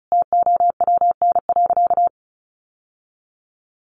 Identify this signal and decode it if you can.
Morse code, heard as a keyed tone, about 34 wpm, 700 Hz. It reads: TOWN.
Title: Sentences from Top 1300 Words 35wpm